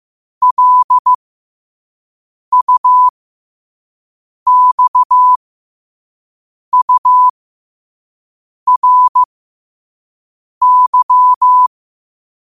Morse code, heard as a keyed tone, about 15 wpm, spaced as for 5 wpm, 1000 Hz. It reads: LUXURY
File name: qotdfiles / 5wpm